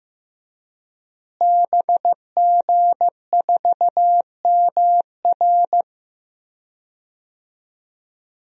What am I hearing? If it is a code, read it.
Morse code, heard as a keyed tone, about 15 wpm, 700 Hz. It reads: BG4MR
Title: DXCC-100 Call Signs - 5 Characters - Repeated 2x - 15wpm